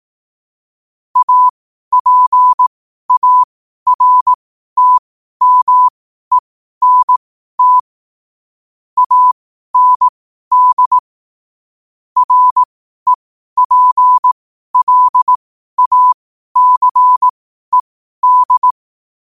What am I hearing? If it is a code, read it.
Morse code, heard as a keyed tone, about 18 wpm, spaced as for 12 wpm, 1000 Hz. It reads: APARTMENT AND REPLACED
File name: qotdfiles / 12wpm